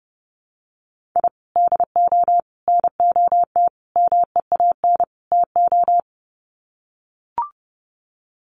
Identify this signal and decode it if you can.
Morse code, heard as a keyed tone, about 30 wpm, 700 Hz. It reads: I DO NOT MEAN TO
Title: Sets of 5 Words 30wpm